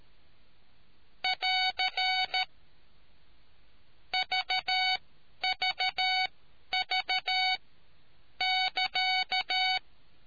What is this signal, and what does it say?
Morse code, heard as a keyed tone, about 13 wpm, spaced as for 9 wpm, 750 Hz. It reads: <AR> VVV<KA>